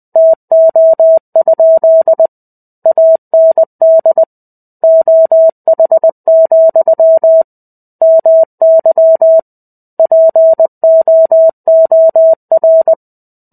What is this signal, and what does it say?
Morse code, heard as a keyed tone, about 20 wpm, 650 Hz. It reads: TO? AND OH, MY POOR
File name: 05-Chapter-05 20 WPM